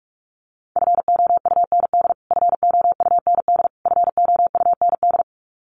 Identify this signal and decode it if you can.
Morse code, heard as a keyed tone, about 44 wpm, 700 Hz. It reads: FOUND FOUND FOUND
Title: Top 500 Words Repeated 45wpm